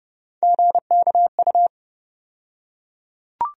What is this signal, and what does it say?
Morse code, heard as a keyed tone, about 31 wpm, 700 Hz. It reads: GKU
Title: USA Call Sign Postfixes - 3 Letters 30wpm